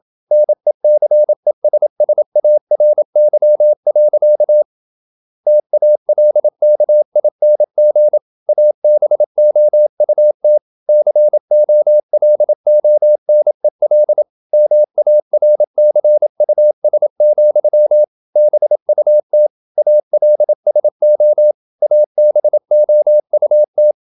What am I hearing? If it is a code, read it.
Morse code, heard as a keyed tone, about 27 wpm, 600 Hz. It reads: NECESSARY. TALKING ABOUT COLONEL MARCUS, BUT ALSO ABOUT
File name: Never 0027wpm